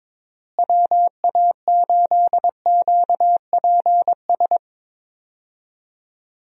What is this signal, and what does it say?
Morse code, heard as a keyed tone, about 22 wpm, 700 Hz. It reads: WA8QPS
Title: USA Call Signs - 6 Characters - Repeated 3x - 22wpm